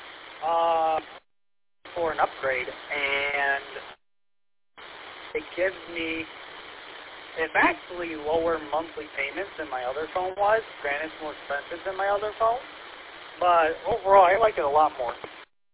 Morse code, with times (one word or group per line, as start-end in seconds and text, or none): none